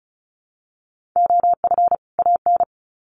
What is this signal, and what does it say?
Morse code, heard as a keyed tone, about 34 wpm, 700 Hz. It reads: OF AN
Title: Sets of 2 Words 35wpm